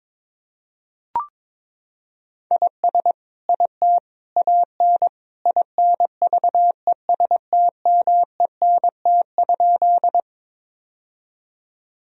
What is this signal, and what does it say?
Morse code, heard as a keyed tone, about 22 wpm, 700 Hz. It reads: IS IT AN INVESTMENT?
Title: Sentences from Top 1000 Words - Review Part 4 - 22wpm